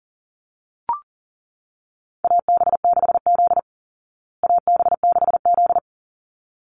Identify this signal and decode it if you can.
Morse code, heard as a keyed tone, about 40 wpm, 700 Hz. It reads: AB6Z AB6Z
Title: Call Signs Repeated 3x - 40wpm